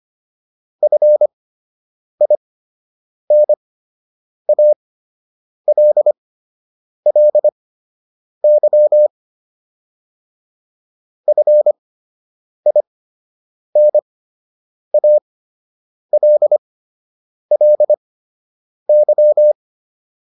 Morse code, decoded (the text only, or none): FINALLY FINALLY